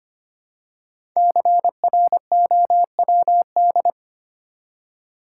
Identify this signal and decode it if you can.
Morse code, heard as a keyed tone, about 25 wpm, 700 Hz. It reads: CROWD
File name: 5 Letter Words - First 3 Letters Given - 25wpm